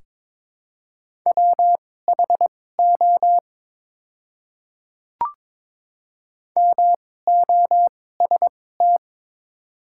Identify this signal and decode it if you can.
Morse code, heard as a keyed tone, about 22 wpm, spaced as for 16 wpm, 700 Hz. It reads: WHO MOST
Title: 2x Character Spacing - Top 100 Words - 22wpm